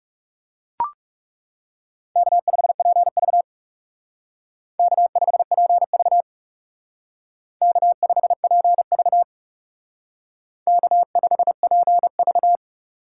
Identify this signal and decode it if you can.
Morse code, heard as a keyed tone, about 34 wpm, 700 Hz. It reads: K5PV K5PV K5PV K5PV
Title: Speed Racing - USA 1x2 Call Signs 30wpm